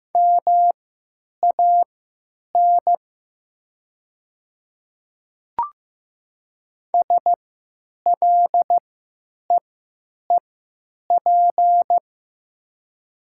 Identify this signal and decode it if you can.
Morse code, heard as a keyed tone, about 15 wpm, spaced as for 9 wpm, 700 Hz. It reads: MAN SLEEP
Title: 3x Character Spacing - Top 500 Words - 15wpm